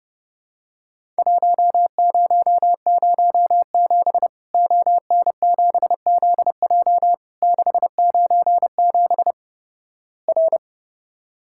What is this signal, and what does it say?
Morse code, heard as a keyed tone, about 30 wpm, 700 Hz. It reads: 1007 ON7ZJ 697 R
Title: Worked All Europe DX Contest - QTC Exchange - 30wpm